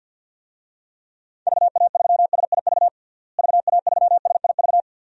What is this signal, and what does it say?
Morse code, heard as a keyed tone, about 52 wpm, 700 Hz. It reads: VA3SIV VA3SIV
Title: DXCC-100 Call Signs - 6 Characters - Repeated 2x - 50wpm